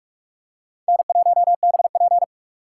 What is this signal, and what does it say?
Morse code, heard as a keyed tone, about 43 wpm, 700 Hz. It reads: N1BP